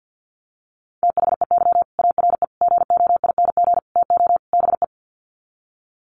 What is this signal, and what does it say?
Morse code, heard as a keyed tone, about 50 wpm, 700 Hz. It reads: THEY ARE GOING TO BE
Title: Sets of 5 Words Encore2 50wpm